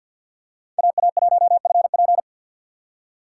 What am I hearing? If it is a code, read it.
Morse code, heard as a keyed tone, about 46 wpm, 700 Hz. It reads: AA1FP
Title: USA Call Signs - 5 Characters - 6x Character Spacing - 50wpm